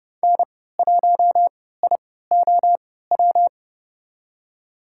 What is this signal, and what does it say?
Morse code, heard as a keyed tone, about 30 wpm, spaced as for 17 wpm, 700 Hz. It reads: N1IOW
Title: USA Call Signs - 5 Characters - 3x Character Spacing - 30wpm